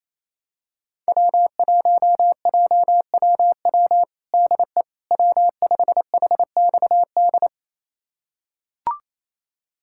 Morse code, heard as a keyed tone, about 28 wpm, 700 Hz. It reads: W1JWW DE W5HXD
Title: USA Callsign DE Callsign 28wpm